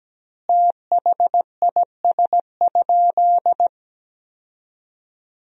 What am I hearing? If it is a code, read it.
Morse code, heard as a keyed tone, about 17 wpm, 700 Hz. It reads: THIS?
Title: 2x Word-Spacing - Sentences from Top 100 Words - 17wpm